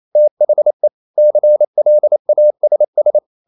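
Morse code, heard as a keyed tone, about 28 wpm, 600 Hz. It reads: THE CLASS
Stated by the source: Morse code practice at 28 wpm, text of Ken Follett's novel Never in English, from CW med Gnister.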